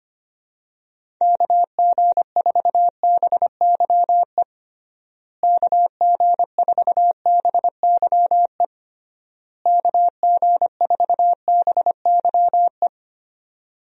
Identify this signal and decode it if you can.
Morse code, heard as a keyed tone, about 25 wpm, 700 Hz. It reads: KG4BYE KG4BYE KG4BYE
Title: USA Call Signs - 6 Characters - Repeated 3x - 25wpm